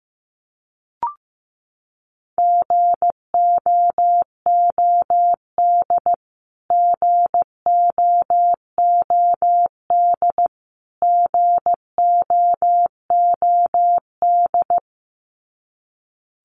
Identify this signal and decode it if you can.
Morse code, heard as a keyed tone, about 15 wpm, 700 Hz. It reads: GOOD GOOD GOOD
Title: Top 200 Words Repeated 15wpm